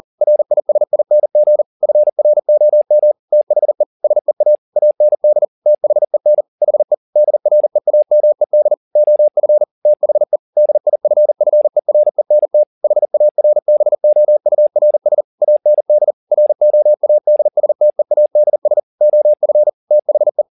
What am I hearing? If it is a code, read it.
Morse code, heard as a keyed tone, about 40 wpm, 600 Hz. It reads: RISING FROM THE SEA AND THEN HE DREAMED OF THE DIFFERENT HARBOURS AND ROADSTEADS OF THE